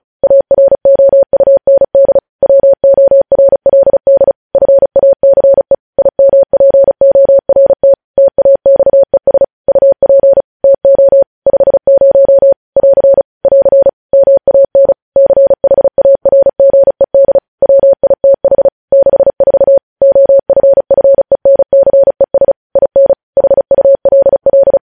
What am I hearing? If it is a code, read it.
Morse code, heard as a keyed tone, about 36 wpm, 550 Hz. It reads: AROUND WORLD FACE IMPORT TAXES UP TO 50 <AR> <AR> MAN CHARGED WITH 64 OFFENCES IN HULL
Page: news headlines in morse code at 35 wpm